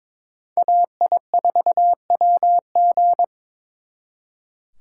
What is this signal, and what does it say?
Morse code, heard as a keyed tone, about 26 wpm, 700 Hz. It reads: AI4WG
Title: Speed Racing - USA Call Signs 22wpm